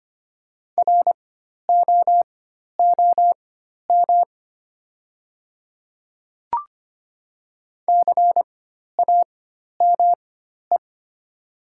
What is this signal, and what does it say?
Morse code, heard as a keyed tone, about 25 wpm, spaced as for 12 wpm, 700 Hz. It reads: ROOM CAME
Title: 4x Character Spacing - Top 300 Words - 25wpm